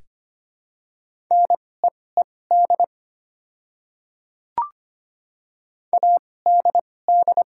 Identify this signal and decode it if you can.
Morse code, heard as a keyed tone, about 25 wpm, spaced as for 18 wpm, 700 Hz. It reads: NEED ADD
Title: 2x Character Spacing - Top 200 Words - 25wpm